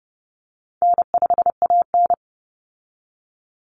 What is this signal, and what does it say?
Morse code, heard as a keyed tone, about 30 wpm, 700 Hz. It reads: N5AN
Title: Call Signs - 30wpm